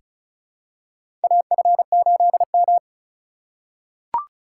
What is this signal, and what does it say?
Morse code, heard as a keyed tone, about 36 wpm, 700 Hz. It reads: AF8M
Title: USA Call Signs - 4 Characters - 3x Character Spacing - 35wpm